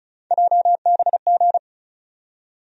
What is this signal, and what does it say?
Morse code, heard as a keyed tone, about 36 wpm, 700 Hz. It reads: JBG
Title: USA Call Sign Postfixes - 3 Letters 35wpm